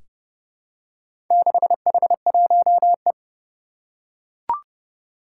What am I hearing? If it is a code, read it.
Morse code, heard as a keyed tone, about 30 wpm, 700 Hz. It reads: 6H1E
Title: Four Challenging Letter-Number Combinations 30wpm